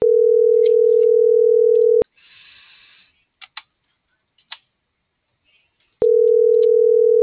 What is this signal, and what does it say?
Ringback tone (North America), 440 + 480 Hz.